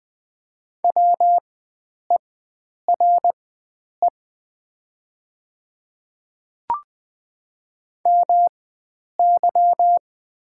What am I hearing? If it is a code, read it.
Morse code, heard as a keyed tone, about 20 wpm, spaced as for 9 wpm, 700 Hz. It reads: WERE MY
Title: 4x Character Spacing - Top 100 Words - 20wpm